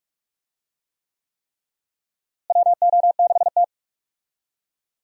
Morse code, heard as a keyed tone, about 46 wpm, 700 Hz. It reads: WO6T
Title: USA Call Signs - 4 Characters - Repeated 3x - 45wpm